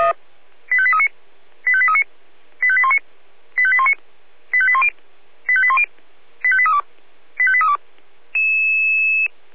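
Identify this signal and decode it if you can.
Five-tone selective call (ZVEI), heard as quick runs of five tones, tones of about 70 ms. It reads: 85839 85839 85819 85819 85810 85810 85832 85832
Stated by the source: ZVEI five-tone call received on an SDR.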